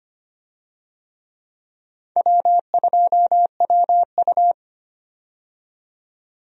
Morse code, heard as a keyed tone, about 25 wpm, 700 Hz. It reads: W2WU
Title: USA Call Signs - 4 Characters - 25wpm